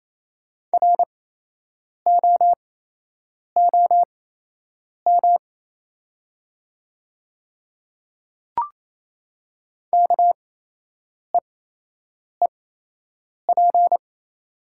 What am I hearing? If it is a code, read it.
Morse code, heard as a keyed tone, about 28 wpm, spaced as for 8 wpm, 690 Hz. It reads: ROOM KEEP
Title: 8x Character Spacing - Top 300 Words - 28wpm